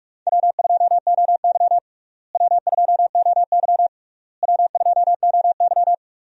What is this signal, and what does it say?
Morse code, heard as a keyed tone, about 42 wpm, 700 Hz. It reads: W2OY W2OY W2OY
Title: USA Call Signs - 4 Characters - Repeated 3x - 45wpm